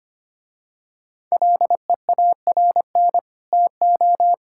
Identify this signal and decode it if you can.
Morse code, heard as a keyed tone, about 25 wpm, 700 Hz. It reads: LEARN TO
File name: Easy Sentences with Hints - 25wpm